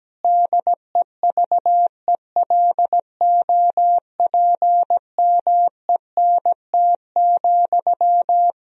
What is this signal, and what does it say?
Morse code, heard as a keyed tone, about 17 wpm, 700 Hz. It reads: DEVELOPMENT,
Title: Sentences from Top 1000 Words - Review Part 4 - 17wpm